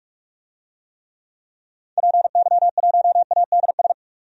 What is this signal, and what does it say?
Morse code, heard as a keyed tone, about 41 wpm, 700 Hz. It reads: PY1ADS
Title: DXCC-100 Call Signs - 6 Characters - 45wpm